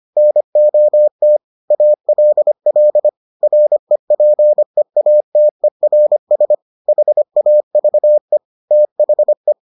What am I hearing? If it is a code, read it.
Morse code, heard as a keyed tone, about 25 wpm, 600 Hz. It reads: NOT ALL REPEATERS HAVE THE